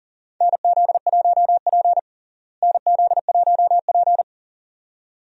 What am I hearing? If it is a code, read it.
Morse code, heard as a keyed tone, about 37 wpm, 700 Hz. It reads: NZ1P NZ1P
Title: USA Call Signs - 4 Characters - Repeated 2x - 40wpm